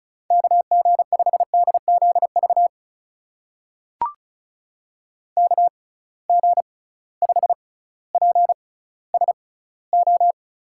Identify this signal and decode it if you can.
Morse code, heard as a keyed tone, about 36 wpm, spaced as for 18 wpm, 700 Hz. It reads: KG5DZV KG5PSO
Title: USA Call Signs - 6 Characters - 6x Character Spacing - 35wpm